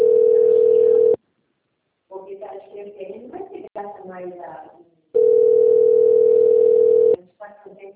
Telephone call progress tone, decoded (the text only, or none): ringback tone